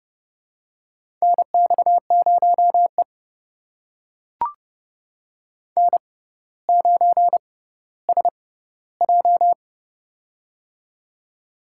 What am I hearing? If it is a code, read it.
Morse code, heard as a keyed tone, about 30 wpm, spaced as for 15 wpm, 700 Hz. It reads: NX0E N9SJ
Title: USA Call Signs - 6x Character Spacing - 30wpm